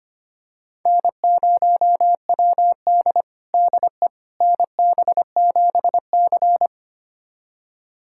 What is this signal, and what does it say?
Morse code, heard as a keyed tone, about 25 wpm, 700 Hz. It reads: N0WD DE NB7C
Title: USA Callsign DE Callsign - 4 Characters - 25wpm